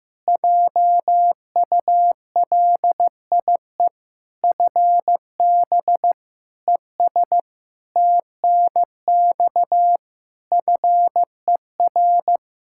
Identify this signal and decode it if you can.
Morse code, heard as a keyed tone, about 15 wpm, 700 Hz. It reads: JULIE FB ES TNX FER
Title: Practice QSO - Exchange 2 - Info - 15wpm